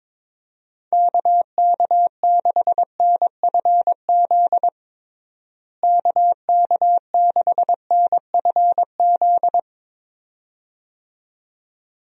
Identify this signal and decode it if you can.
Morse code, heard as a keyed tone, about 22 wpm, 700 Hz. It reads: KK6NFZ KK6NFZ
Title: USA Call Signs - 6 Characters - Repeated 2x - 22wpm